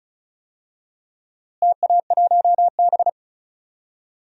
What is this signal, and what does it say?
Morse code, heard as a keyed tone, about 34 wpm, 700 Hz. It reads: TA1B
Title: DXCC-100 Call Signs - 4 Characters - Repeated 3x - 35wpm